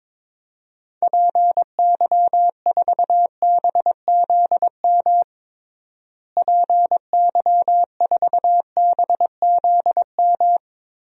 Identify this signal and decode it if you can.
Morse code, heard as a keyed tone, about 22 wpm, 700 Hz. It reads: PY4BZM PY4BZM
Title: DXCC-100 Call Signs - 6 Characters - Repeated 3x - 22wpm